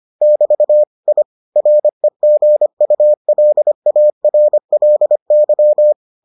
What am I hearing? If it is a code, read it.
Morse code, heard as a keyed tone, about 25 wpm, 600 Hz. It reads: <BT> I REGULARLY